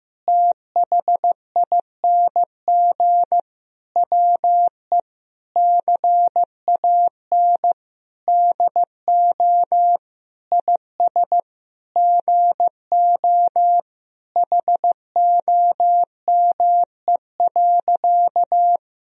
Morse code, heard as a keyed tone, about 15 wpm, 700 Hz. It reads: THING WE CAN DO IS GO HOME.
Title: Sentences from Top 200 Words and Abbreviations 15wpm